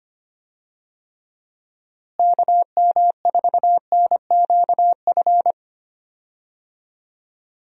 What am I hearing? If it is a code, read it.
Morse code, heard as a keyed tone, about 25 wpm, 700 Hz. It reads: KM4NQF
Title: USA Call Signs - 6 Characters - Repeated 3x - 25wpm